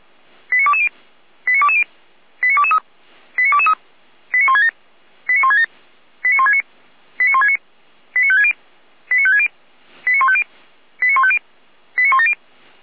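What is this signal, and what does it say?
Five-tone selective call (ZVEI), heard as quick runs of five tones, tones of about 70 ms. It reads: 89220 89220 89223 89223 89167 89167 89169 89169 89560 89560 89150 89150 89170
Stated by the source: ZVEI five-tone selective call received via SDR.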